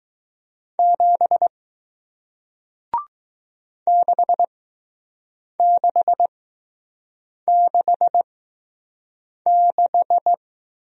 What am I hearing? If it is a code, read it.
Morse code, heard as a keyed tone, about 20 wpm, spaced as for 6 wpm, 700 Hz. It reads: 7 6666